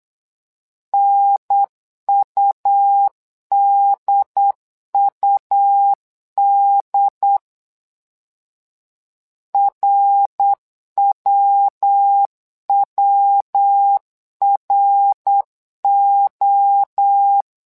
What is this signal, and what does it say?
Morse code, heard as a keyed tone, about 8 wpm, 800 Hz. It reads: NUDUD RWWRO
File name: LEZIONI / 3 MWRO / 3 MWRO 8wpm 2606